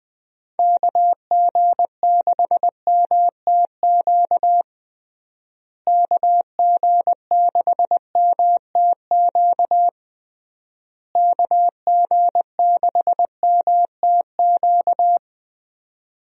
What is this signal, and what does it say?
Morse code, heard as a keyed tone, about 20 wpm, 700 Hz. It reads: KG6MTQ KG6MTQ KG6MTQ
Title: USA Call Signs - 6 Characters - Repeated 3x - 20wpm